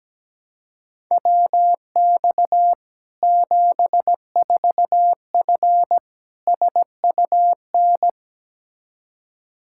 Morse code, heard as a keyed tone, about 17 wpm, 700 Hz. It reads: WX 74F SUN
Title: Practice QSO - Exchange 2 - Info - 17wpm